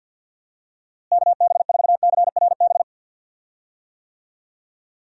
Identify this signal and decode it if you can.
Morse code, heard as a keyed tone, about 45 wpm, 700 Hz. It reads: KB4CRB
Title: USA Call Signs - 6 Characters - Repeated 2x - 50wpm